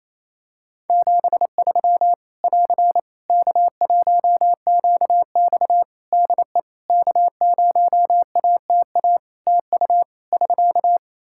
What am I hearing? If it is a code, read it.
Morse code, heard as a keyed tone, about 27 wpm, 700 Hz. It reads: 73 <AR> K1QX DE K0ATA TU <SK>
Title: Practice QSO - Exchange 4 - Ending - 28wpm